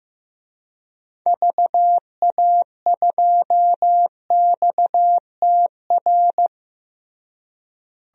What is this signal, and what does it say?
Morse code, heard as a keyed tone, about 16 wpm, 700 Hz. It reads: VA2XTR
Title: DXCC-100 Call Signs - 6 Characters - Repeated 3x - 15wpm